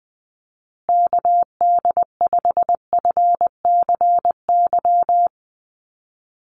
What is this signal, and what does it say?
Morse code, heard as a keyed tone, about 20 wpm, 700 Hz. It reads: KD5FCY
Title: USA Call Signs Repeated 2x 20wpm